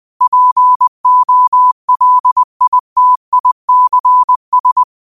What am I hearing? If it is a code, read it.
Morse code, heard as a keyed tone, about 20 wpm, 1000 Hz. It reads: POLITICS
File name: qotdfiles / 20wpm